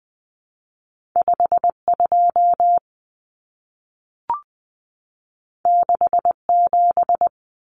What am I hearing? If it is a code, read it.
Morse code, heard as a keyed tone, about 20 wpm, 700 Hz. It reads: 52 67
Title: Common Cut Numbers - 2-Digit 20wpm